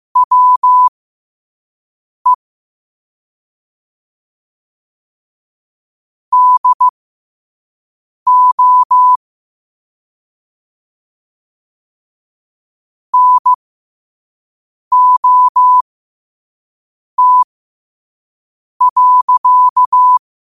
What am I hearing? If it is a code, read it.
Morse code, heard as a keyed tone, about 15 wpm, spaced as for 5 wpm, 1000 Hz. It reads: WE DO NOT.
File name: qotdfiles / 5wpm